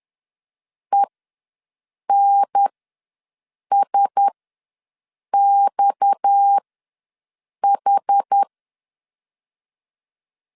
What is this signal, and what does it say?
Morse code, heard as a keyed tone, about 10 wpm, spaced as for 6 wpm, 800 Hz. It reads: ENSXH
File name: LEZIONI / 6 PKX0 / 6 PKX ws 10wpm 3706